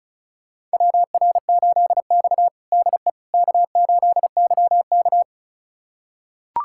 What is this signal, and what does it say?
Morse code, heard as a keyed tone, about 36 wpm, 700 Hz. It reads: WR8X DE K8YK
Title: USA Callsign DE Callsign - 4 Characters - 35wpm